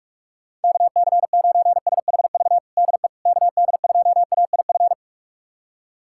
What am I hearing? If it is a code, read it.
Morse code, heard as a keyed tone, about 47 wpm, 700 Hz. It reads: KC9SHV DE KD2AIF
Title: USA Callsign DE Callsign - 6 Characters - 45wpm